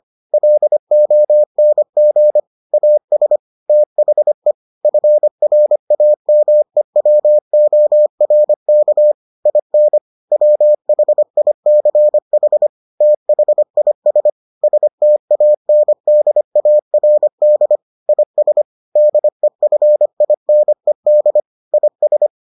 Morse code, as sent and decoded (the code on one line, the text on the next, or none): .-.. --- -. --. / .- ... / - .... / ..-. .-. .- -- . .-- --- .-. -.- / .. -. / .-- .... .. -.-. .... / - .... .. ... / ... - .- -. -.. .- .-. -.. / .. ... / -.. ..-. .. -. -.. / .. ...
LONG AS THE FRAMEWORK IN WHICH THIS STANDARD IS DEFINED IS